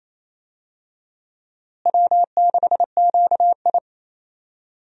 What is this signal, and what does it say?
Morse code, heard as a keyed tone, about 28 wpm, 700 Hz. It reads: W6QI
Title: USA Call Signs - 4 Characters - Repeated 3x - 28wpm